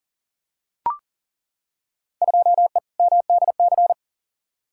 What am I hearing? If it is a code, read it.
Morse code, heard as a keyed tone, about 40 wpm, 700 Hz. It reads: 2E MDC
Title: Field Day Exchanges - 40wpm